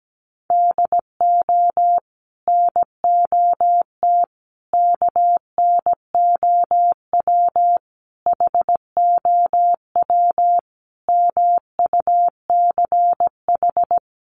Morse code, heard as a keyed tone, about 17 wpm, 700 Hz. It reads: DO NOT KNOW HOW MUCH
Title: Sets of 5 Words Encore 17wpm